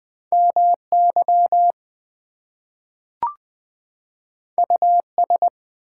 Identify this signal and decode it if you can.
Morse code, heard as a keyed tone, about 20 wpm, 700 Hz. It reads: MY US
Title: Two Letter Words - 20wpm